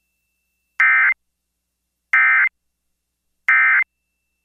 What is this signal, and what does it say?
Emergency Alert System (SAME) tones, heard as harsh buzzing bursts. It is heard from 0.8 to 3.8 s.